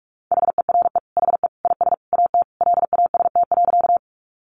Morse code, heard as a keyed tone, about 44 wpm, 700 Hz. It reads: HERE HE IS AT LAST.
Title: Sentences from Top 300 Words 45wpm